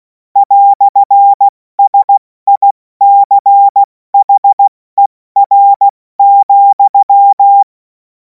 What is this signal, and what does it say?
Morse code, heard as a keyed tone, about 16 wpm, 800 Hz. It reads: "SICHER,
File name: Morsekurs / Witze / Witz 07. April 2026 70 F80